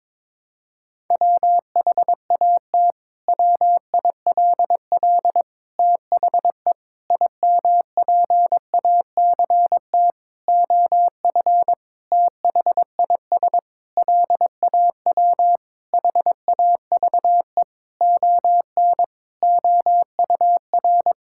Morse code, heard as a keyed tone, about 22 wpm, 700 Hz. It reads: WHAT WILL THE IMPACT OF THIS LAW HAVE ON OUR
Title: Sentences from Top 1000 Words - Review Part 6 - 22wpm